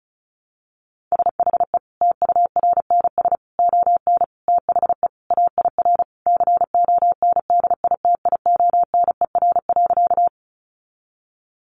Morse code, heard as a keyed tone, about 34 wpm, 700 Hz. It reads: SHE TURNS ON THE AIR CONDITIONER.
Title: Additional Easy Phrases 35wpm